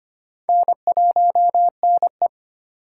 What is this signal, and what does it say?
Morse code, heard as a keyed tone, about 25 wpm, 700 Hz. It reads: N1NE
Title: USA Call Signs - 2x Character Spacing - 25wpm